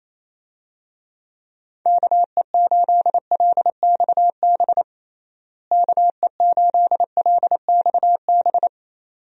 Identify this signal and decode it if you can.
Morse code, heard as a keyed tone, about 28 wpm, 700 Hz. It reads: KE8LXB KE8LXB
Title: USA Call Signs - 6 Characters - Repeated 2x - 28wpm